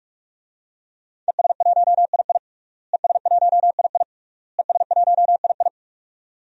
Morse code, heard as a keyed tone, about 42 wpm, 700 Hz. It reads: ES1II ES1II ES1II
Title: DXCC-100 Call Signs - 5 Characters - Repeated 3x - 45wpm